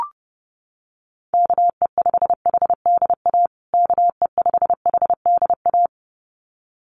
Morse code, heard as a keyed tone, about 30 wpm, 700 Hz. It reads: KE5HDA KE5HDA
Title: USA Call Signs Repeated 2x 30wpm